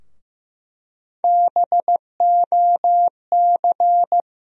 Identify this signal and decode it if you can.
Morse code, heard as a keyed tone, about 15 wpm, 700 Hz. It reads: BOC
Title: USA Call Sign Postfixes - 3 Letters 15wpm